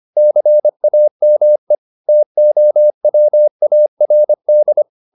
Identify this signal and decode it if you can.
Morse code, heard as a keyed tone, about 25 wpm, 600 Hz. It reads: CAME TOWARD
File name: the old man and the sea part 1 25wpm